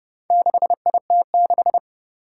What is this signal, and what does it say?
Morse code, heard as a keyed tone, about 30 wpm, 700 Hz. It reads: 6IT6
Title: Four Challenging Letter-Number Combinations 30wpm